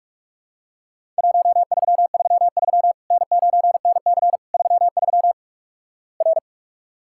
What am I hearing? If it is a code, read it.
Morse code, heard as a keyed tone, about 48 wpm, 700 Hz. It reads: 1333 N9NC 33 R
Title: Worked All Europe DX Contest - QTC Exchange - 45wpm